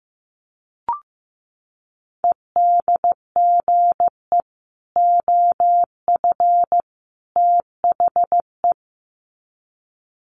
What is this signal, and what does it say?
Morse code, heard as a keyed tone, about 15 wpm, 700 Hz. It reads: EDGE OF THE
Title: Sets of 3 Words 15wpm